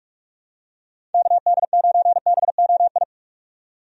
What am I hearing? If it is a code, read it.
Morse code, heard as a keyed tone, about 44 wpm, 700 Hz. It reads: KD9BOI
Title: USA Call Signs - 6 Characters - 45wpm